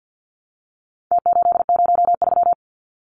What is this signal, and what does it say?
Morse code, heard as a keyed tone, about 50 wpm, 700 Hz. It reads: T803
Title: Numbers - Contest Style - 4-Digit 50wpm